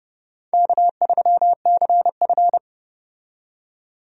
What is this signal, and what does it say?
Morse code, heard as a keyed tone, about 30 wpm, 700 Hz. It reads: K3CF
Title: USA Call Signs - 4 Characters - 30wpm